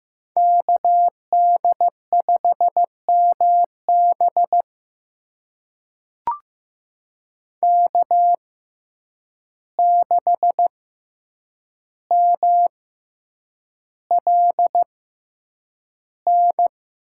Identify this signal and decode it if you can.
Morse code, heard as a keyed tone, about 15 wpm, spaced as for 8 wpm, 700 Hz. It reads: KD5MB K6MLN
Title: USA Call Signs - 5 Characters - 6x Character Spacing - 15wpm